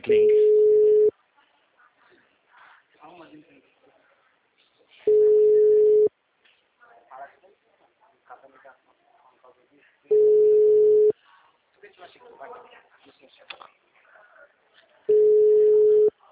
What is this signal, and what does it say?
Ringback tone (Europe, 425 Hz), with speech.
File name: out- -2007